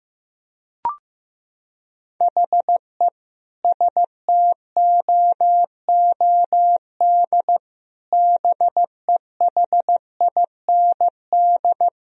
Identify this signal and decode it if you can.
Morse code, heard as a keyed tone, about 16 wpm, 700 Hz. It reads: HE STOOD BEHIND
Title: Sentences from Top 1000 Words - Review Part 2 - 15wpm